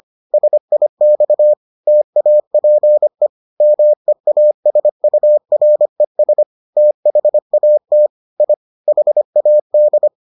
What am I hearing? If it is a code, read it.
Morse code, heard as a keyed tone, about 25 wpm, 600 Hz. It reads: SIX TAPE MEASURES THAT I HAD